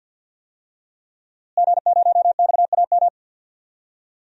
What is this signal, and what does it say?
Morse code, heard as a keyed tone, about 52 wpm, 700 Hz. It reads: G0XAM
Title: DXCC-100 Call Signs - 5 Characters - Repeated 3x - 50wpm